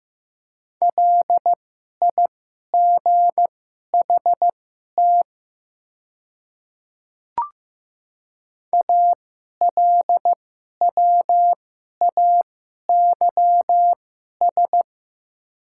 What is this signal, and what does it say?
Morse code, heard as a keyed tone, about 15 wpm, spaced as for 11 wpm, 700 Hz. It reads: LIGHT ALWAYS
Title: 2x Character Spacing - Top 300 Words - 15wpm